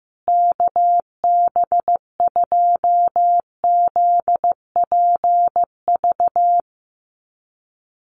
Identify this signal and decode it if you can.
Morse code, heard as a keyed tone, about 15 wpm, 700 Hz. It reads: KB2ZPV
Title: USA Call Signs Repeated 2x 15wpm